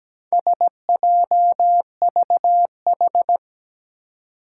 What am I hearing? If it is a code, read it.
Morse code, heard as a keyed tone, about 17 wpm, 700 Hz. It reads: SJVH
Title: Four Challenging Letter-Number Combinations 17wpm